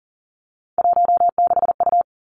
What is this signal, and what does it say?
Morse code, heard as a keyed tone, about 38 wpm, 700 Hz. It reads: 16U
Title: Three Challenging Letter-Number Combinations 40wpm